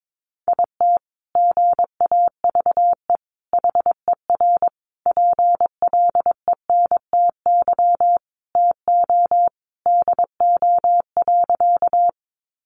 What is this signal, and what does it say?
Morse code, heard as a keyed tone, about 22 wpm, 700 Hz. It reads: IT GAVE HER PLENTY TO DO.
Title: Sentences from Top 1300 Words 22wpm